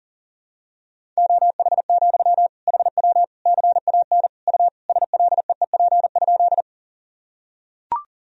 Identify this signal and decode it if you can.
Morse code, heard as a keyed tone, about 40 wpm, 700 Hz. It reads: OH, HW CAN U SLEEP?
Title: Sentences from Top 500 Words and Abbreviations 40wpm